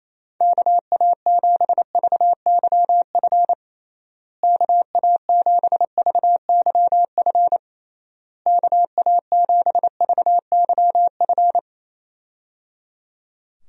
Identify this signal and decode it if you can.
Morse code, heard as a keyed tone, about 27 wpm, 700 Hz. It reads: KA7VYF KA7VYF KA7VYF
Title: USA Call Signs - 6 Characters - Repeated 3x - 28wpm